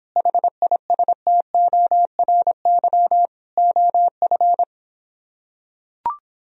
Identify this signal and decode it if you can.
Morse code, heard as a keyed tone, about 26 wpm, 700 Hz. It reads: HISTORY OF